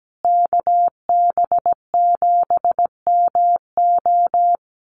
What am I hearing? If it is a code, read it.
Morse code, heard as a keyed tone, about 17 wpm, 700 Hz. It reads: KB7MO
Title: USA Call Signs Repeated 2x 17wpm